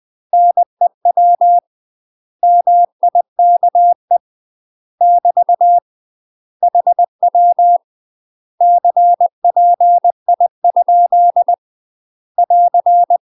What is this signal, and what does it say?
Morse code, heard as a keyed tone, about 20 wpm, 700 Hz. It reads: NEW MIKE <BT> HW CPI? <AR>
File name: UK MASTER QSOs TRAINING / 20WPM / UK MASTER QSOs 0023